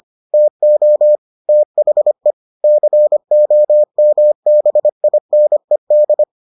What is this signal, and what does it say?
Morse code, heard as a keyed tone, about 25 wpm, 600 Hz. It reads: TO THE COMBINED